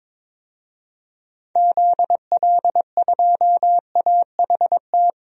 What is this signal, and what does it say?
Morse code, heard as a keyed tone, about 21 wpm, 700 Hz. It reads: ZL2AHT
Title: DXCC-100 Call Signs - 6 Characters - 22wpm